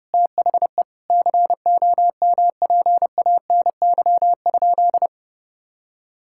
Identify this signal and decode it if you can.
Morse code, heard as a keyed tone, about 30 wpm, 700 Hz. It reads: THE COMPANY?
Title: Sentences from Top 1600 Words 30wpm